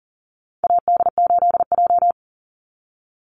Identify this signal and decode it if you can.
Morse code, heard as a keyed tone, about 40 wpm, 700 Hz. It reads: AD8J
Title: Call Signs - 40wpm